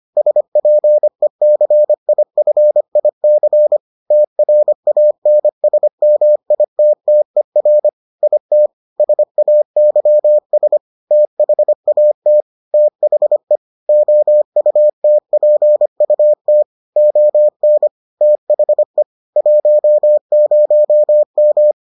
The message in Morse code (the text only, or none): SPECIFIC TRANSMITTER IT SAYS THAT THE OUTPUT ON THE 10M